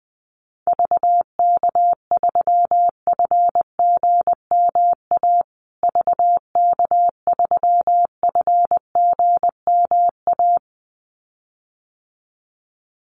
Morse code, heard as a keyed tone, about 20 wpm, 700 Hz. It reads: VK3FGMA VK3FGMA
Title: USA Call Signs Repeated 2x 20wpm